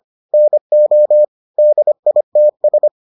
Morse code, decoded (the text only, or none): NO DITS